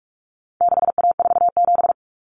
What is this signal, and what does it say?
Morse code, heard as a keyed tone, about 44 wpm, 700 Hz. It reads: 6A47
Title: Common Cut Numbers - 4-Digit 45wpm